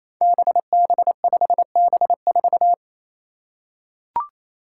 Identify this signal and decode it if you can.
Morse code, heard as a keyed tone, about 29 wpm, 700 Hz. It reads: BB5B4
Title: Five Challenging Letter-Number Combinations 28wpm